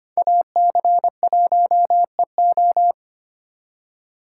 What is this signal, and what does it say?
Morse code, heard as a keyed tone, about 32 wpm, 700 Hz. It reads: AC1EO